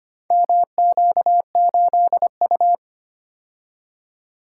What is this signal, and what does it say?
Morse code, heard as a keyed tone, about 25 wpm, 700 Hz. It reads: MQ8U
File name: practice / 25wpm / Four Letter-Number Combinations - 25wpm